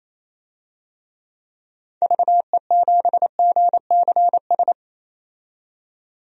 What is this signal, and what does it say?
Morse code, heard as a keyed tone, about 28 wpm, 700 Hz. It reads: VE7GCS
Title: DXCC-100 Call Signs - 6 Characters - 28wpm